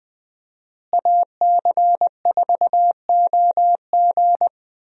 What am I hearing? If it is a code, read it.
Morse code, heard as a keyed tone, about 20 wpm, 700 Hz. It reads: AC4OG
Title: USA Call Signs - 5 Characters - Repeated 3x - 20wpm